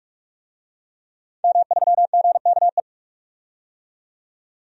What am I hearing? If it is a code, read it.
Morse code, heard as a keyed tone, about 46 wpm, 700 Hz. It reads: M3GKE
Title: DXCC-100 Call Signs - 5 Characters - 45wpm